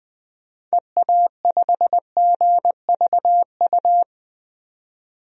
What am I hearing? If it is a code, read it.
Morse code, heard as a keyed tone, about 20 wpm, 700 Hz. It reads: EA5GVU
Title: DXCC-100 Call Signs - 6 Characters - 20wpm